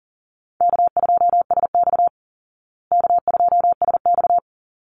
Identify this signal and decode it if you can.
Morse code, heard as a keyed tone, about 40 wpm, 700 Hz. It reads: K2SX K2SX
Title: Call Signs Repeated 2x - 40wpm